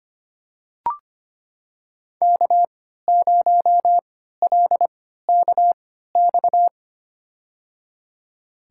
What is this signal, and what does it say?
Morse code, heard as a keyed tone, about 25 wpm, spaced as for 14 wpm, 700 Hz. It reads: K0LKX
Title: USA Call Signs - 5 Characters - 3x Character Spacing - 25wpm